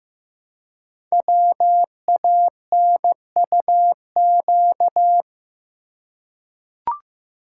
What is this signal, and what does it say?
Morse code, heard as a keyed tone, about 15 wpm, 700 Hz. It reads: WANUQ
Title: Code Groups - 5 Letter:Numbers - 15wpm